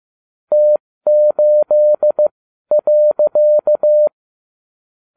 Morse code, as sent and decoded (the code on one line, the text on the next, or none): - ---.. .-.-.-
T8.